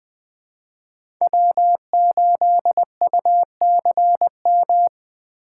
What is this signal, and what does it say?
Morse code, heard as a keyed tone, about 20 wpm, 700 Hz. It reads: W8UCM
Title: USA Call Signs - 5 Characters - 2x Character Spacing - 20wpm